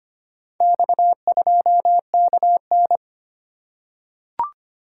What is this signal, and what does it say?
Morse code, heard as a keyed tone, about 25 wpm, 700 Hz. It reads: X2KN
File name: practice / 25wpm / Four Letter-Number Combinations - 25wpm